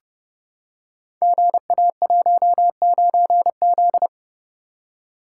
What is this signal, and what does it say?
Morse code, heard as a keyed tone, about 30 wpm, 700 Hz. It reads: GA19Z
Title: Code Groups - 5 Letter:Numbers - 30wpm